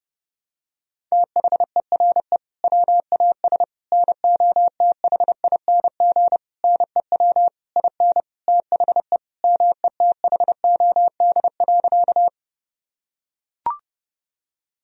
Morse code, heard as a keyed tone, about 30 wpm, 700 Hz. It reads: THERE WAS NOTHING NEW IN THE METHOD.
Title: Sentences from Top 1000 Words - Review Part 3 - 30wpm